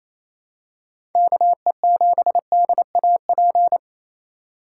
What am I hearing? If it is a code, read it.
Morse code, heard as a keyed tone, about 27 wpm, 700 Hz. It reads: KE7DAP
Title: USA Call Signs - 4x Character Spacing - 28wpm